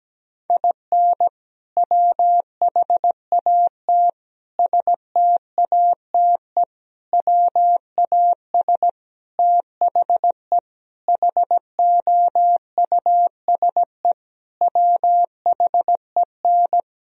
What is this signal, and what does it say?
Morse code, heard as a keyed tone, about 17 wpm, 700 Hz. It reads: IN WHAT STATE WAS THE HOUSE WHEN
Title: Sentences from Top 1000 Words - Review Part 2 - 17wpm